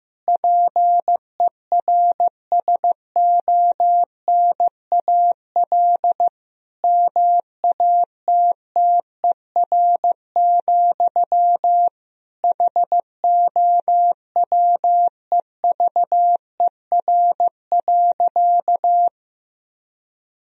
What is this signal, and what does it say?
Morse code, heard as a keyed tone, about 15 wpm, 700 Hz. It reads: PERSONAL MATTER, HOWEVER.
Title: Sentences from Top 1000 Words - Review Part 4 - 15wpm